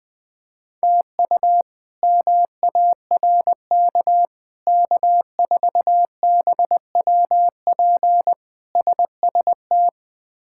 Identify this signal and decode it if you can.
Morse code, heard as a keyed tone, about 21 wpm, 700 Hz. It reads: TU MARK K4BWP SST
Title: K1USN SST QSO Practice - 20wpm